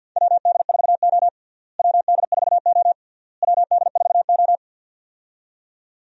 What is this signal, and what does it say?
Morse code, heard as a keyed tone, about 53 wpm, 700 Hz. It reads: WD4O WD4O WD4O
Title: USA Call Signs - 4 Characters - Repeated 3x - 50wpm